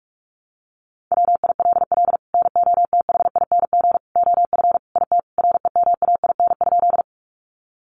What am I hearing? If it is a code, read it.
Morse code, heard as a keyed tone, about 44 wpm, 700 Hz. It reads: WILL NOTHING OF IT REMAIN?
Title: Sentences from Top 1200 Words 45wpm